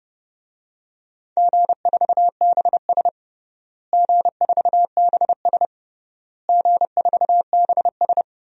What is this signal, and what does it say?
Morse code, heard as a keyed tone, about 32 wpm, 700 Hz. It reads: G4BS G4BS G4BS
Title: DXCC-100 Call Signs - 4 Characters - Repeated 3x - 30wpm